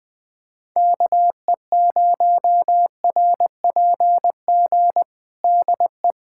Morse code, heard as a keyed tone, about 20 wpm, 700 Hz. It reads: KE0RPG DE
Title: USA Callsign DE Callsign - 6 Characters - 20wpm